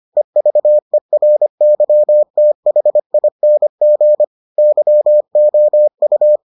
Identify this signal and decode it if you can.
Morse code, heard as a keyed tone, about 25 wpm, 600 Hz. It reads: EVERYTHING YOU